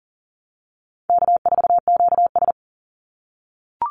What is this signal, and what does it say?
Morse code, heard as a keyed tone, about 40 wpm, 700 Hz. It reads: K4QS
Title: NA SOTA Activator Callsigns 40wpm